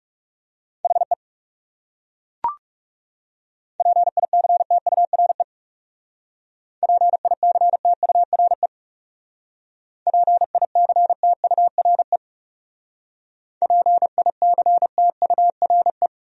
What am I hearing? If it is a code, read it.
Morse code, heard as a keyed tone, about 37 wpm, 700 Hz. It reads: HE PICTURE PICTURE PICTURE PICTURE
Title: Speed-Racing Top 200 Words 30wpm